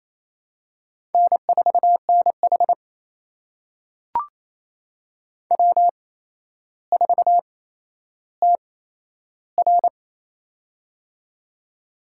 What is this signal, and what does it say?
Morse code, heard as a keyed tone, about 27 wpm, spaced as for 12 wpm, 700 Hz. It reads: N4NH W4TR
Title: USA Call Signs - 4 Characters - 8x Character Spacing - 28wpm